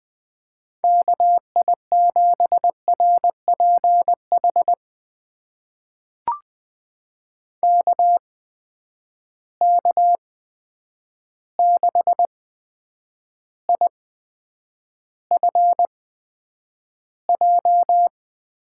Morse code, heard as a keyed tone, about 20 wpm, spaced as for 9 wpm, 700 Hz. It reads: KI7RPH KK6IFJ
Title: USA Call Signs - 6 Characters - 8x Character Spacing - 20wpm